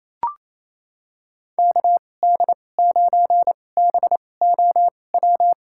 Morse code, heard as a keyed tone, about 28 wpm, spaced as for 20 wpm, 700 Hz. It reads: KD9BOW
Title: USA Call Signs - 2x Character Spacing - 28wpm